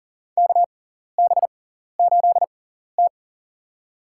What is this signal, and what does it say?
Morse code, heard as a keyed tone, about 40 wpm, spaced as for 14 wpm, 700 Hz. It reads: KB8T
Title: USA Call Signs - 4 Characters - 6x Character Spacing - 40wpm